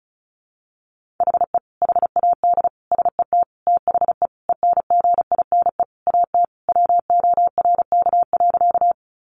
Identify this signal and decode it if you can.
Morse code, heard as a keyed tone, about 34 wpm, 700 Hz. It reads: HE HAD SET THE ENGINE AT WORK.
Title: Sentences from Top 1500 Words 35wpm